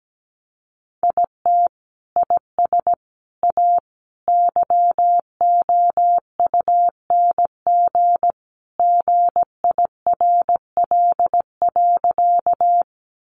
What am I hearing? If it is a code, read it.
Morse code, heard as a keyed tone, about 17 wpm, 700 Hz. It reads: IT IS A YOUNG GIRL.
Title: Sentences from Top 400 Words 17wpm